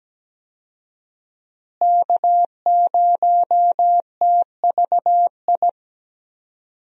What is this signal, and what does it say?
Morse code, heard as a keyed tone, about 17 wpm, 700 Hz. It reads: K0TVI
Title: USA Call Signs - 5 Characters - Repeated 3x - 17wpm